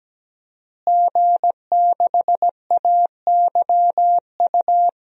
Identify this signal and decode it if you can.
Morse code, heard as a keyed tone, about 17 wpm, 700 Hz. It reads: G6AYU
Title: DXCC-100 Call Signs - 5 Characters - Repeated 3x - 17wpm